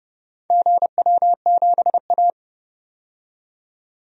Morse code, heard as a keyed tone, about 31 wpm, 700 Hz. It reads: GW7A
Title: DXCC-100 Call Signs - 4 Characters - 30wpm